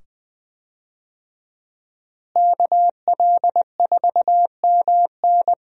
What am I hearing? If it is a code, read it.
Morse code, heard as a keyed tone, about 20 wpm, 700 Hz. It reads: KL4MN
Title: USA Call Signs - 5 Characters - Repeated 2x - 20wpm